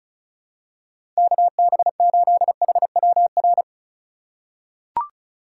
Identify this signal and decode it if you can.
Morse code, heard as a keyed tone, about 36 wpm, 700 Hz. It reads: KB8HWR
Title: USA Call Signs - 6 Characters - 4x Character Spacing - 35wpm